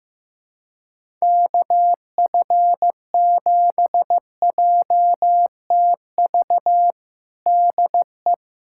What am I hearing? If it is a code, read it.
Morse code, heard as a keyed tone, about 15 wpm, 700 Hz. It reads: KF7JTV DE
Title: USA Callsign DE Callsign 15wpm